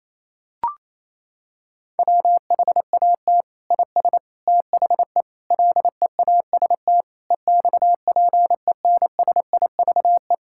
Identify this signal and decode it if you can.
Morse code, heard as a keyed tone, about 28 wpm, 700 Hz. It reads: WHAT IS THE LEAST EXPENSIVE
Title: Sentences from Top 1000 Words - Review Part 6 - 28wpm